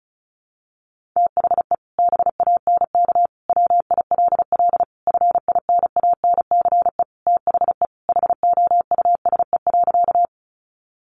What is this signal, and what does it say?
Morse code, heard as a keyed tone, about 34 wpm, 700 Hz. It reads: THE BANK WILL FINANCE THE HOUSE.
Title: Sentences from Top 1500 Words 35wpm